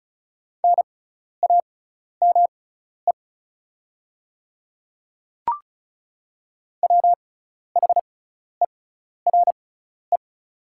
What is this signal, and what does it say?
Morse code, heard as a keyed tone, about 36 wpm, spaced as for 12 wpm, 700 Hz. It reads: NAME WHERE
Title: 6x Character Spacing - Top 200 Words - 35wpm